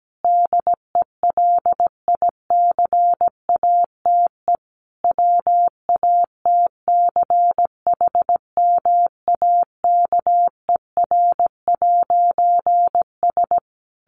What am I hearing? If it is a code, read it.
Morse code, heard as a keyed tone, about 17 wpm, 700 Hz. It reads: DELICATE WATCHMAKER'S
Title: A Sound Of Thunder 17wpm